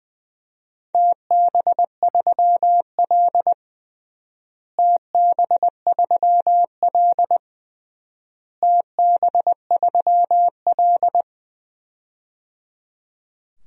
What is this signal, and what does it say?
Morse code, heard as a keyed tone, about 21 wpm, 700 Hz. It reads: TB3L TB3L TB3L
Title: DXCC-100 Call Signs - 4 Characters - Repeated 3x - 20wpm